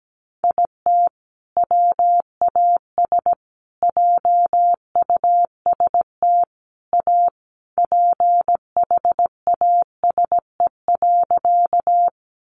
Morse code, heard as a keyed tone, about 17 wpm, 700 Hz. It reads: IT WAS JUST A PHASE.
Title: Sentences from Top 1200 Words 17wpm